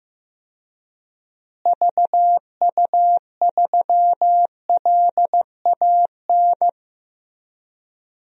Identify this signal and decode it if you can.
Morse code, heard as a keyed tone, about 15 wpm, 700 Hz. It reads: VU3LAN
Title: DXCC-100 Call Signs - 6 Characters - Repeated 3x - 15wpm